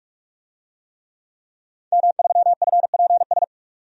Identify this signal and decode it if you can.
Morse code, heard as a keyed tone, about 46 wpm, 700 Hz. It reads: M3FPS
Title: DXCC-100 Call Signs - 5 Characters - 45wpm